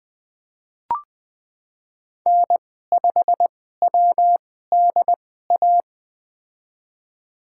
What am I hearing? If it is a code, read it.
Morse code, heard as a keyed tone, about 20 wpm, spaced as for 14 wpm, 700 Hz. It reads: N5WDA
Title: USA Call Signs - 5 Characters - 2x Character Spacing - 20wpm